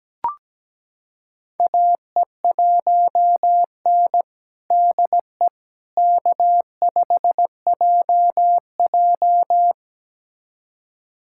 Morse code, heard as a keyed tone, about 17 wpm, 700 Hz. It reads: AE1N DE K5JJ
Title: USA Callsign DE Callsign - 4 Characters - 17wpm